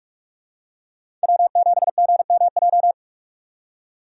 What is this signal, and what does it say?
Morse code, heard as a keyed tone, about 43 wpm, 690 Hz. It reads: W7GMJ
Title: USA Call Signs - 5 Characters - 6x Character Spacing - 45wpm